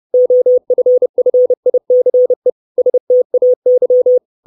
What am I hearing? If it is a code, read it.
Morse code, heard as a keyed tone, about 30 wpm, 500 Hz. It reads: OFFICE STAY
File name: 632 Words 30WPM